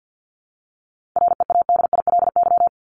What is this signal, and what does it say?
Morse code, heard as a keyed tone, about 50 wpm, 700 Hz. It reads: READILY
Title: Top 1751-2000 Words 50wpm